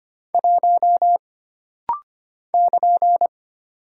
Morse code, heard as a keyed tone, about 25 wpm, 700 Hz. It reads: 1 <KN>
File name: Single Character (Letter:Number:Punctuation:Prosign) - Rapid-Fire - 25wpm